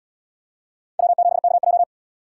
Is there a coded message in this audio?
Yes: Morse code from 1.0 to 1.8 s, heard as a keyed tone, about 71 wpm, 700 Hz. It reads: W7KJ